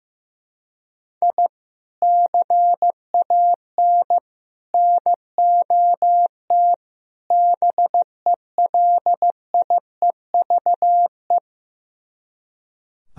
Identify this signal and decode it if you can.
Morse code, heard as a keyed tone, about 15 wpm, 700 Hz. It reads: I CAN NOT BELIEVE
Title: Speed-Racing - Sets of 4 Words 15wpm